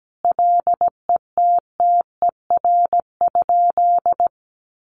Morse code, heard as a keyed tone, about 17 wpm, 700 Hz. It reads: LETTER?
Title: Sentences from Top 600 Words 17wpm